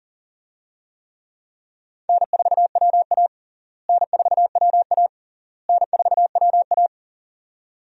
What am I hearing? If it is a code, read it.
Morse code, heard as a keyed tone, about 40 wpm, 700 Hz. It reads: N4WA N4WA N4WA
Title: USA Call Signs - 4 Characters - Repeated 3x - 40wpm